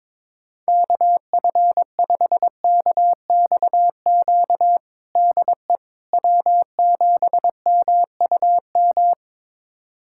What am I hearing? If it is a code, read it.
Morse code, heard as a keyed tone, about 22 wpm, 700 Hz. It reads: KF5KXQ DE W7MUM
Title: USA Callsign DE Callsign 22wpm